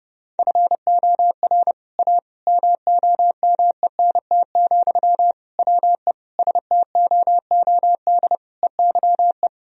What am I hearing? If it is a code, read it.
Morse code, heard as a keyed tone, about 30 wpm, 700 Hz. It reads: FOR A MOMENT, WE STOOD EYE
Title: Sentences from Top 1000 Words - Review Part 2 - 30wpm